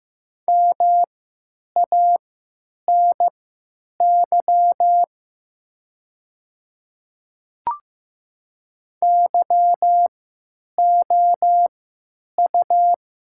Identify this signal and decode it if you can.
Morse code, heard as a keyed tone, about 15 wpm, spaced as for 9 wpm, 700 Hz. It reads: MANY YOU
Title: 3x Character Spacing - Top 100 Words - 15wpm